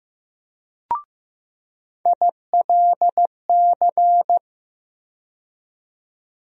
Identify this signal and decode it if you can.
Morse code, heard as a keyed tone, about 15 wpm, 700 Hz. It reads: ILC
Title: USA Call Sign Postfixes - 3 Letters 15wpm